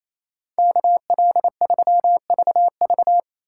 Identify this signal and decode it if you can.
Morse code, heard as a keyed tone, about 35 wpm, 700 Hz. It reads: KL3VV
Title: Speed Racing - USA 2x2 Call Signs 28wpm